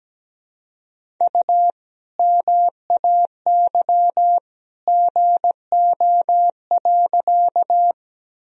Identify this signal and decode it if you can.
Morse code, heard as a keyed tone, about 17 wpm, 700 Hz. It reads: U MAY GO.
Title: Sentences from Top 100 Words and Abbreviations 17wpm